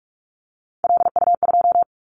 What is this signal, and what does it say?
Morse code, heard as a keyed tone, about 44 wpm, 700 Hz. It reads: LU2
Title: Callsign Prefixes for Top 100 DXCC - 45wpm